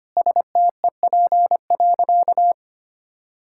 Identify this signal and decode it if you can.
Morse code, heard as a keyed tone, about 25 wpm, 700 Hz. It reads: STEP.
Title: Sentences from Top 1600 Words 25wpm